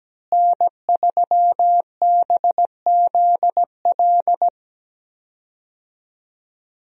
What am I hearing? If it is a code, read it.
Morse code, heard as a keyed tone, about 17 wpm, 700 Hz. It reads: N3BZL
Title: USA Call Signs - 5 Characters - Repeated 3x - 17wpm